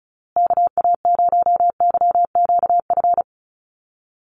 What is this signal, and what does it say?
Morse code, heard as a keyed tone, about 34 wpm, 700 Hz. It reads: KA0YQF
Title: USA Call Signs Repeated 2x 35wpm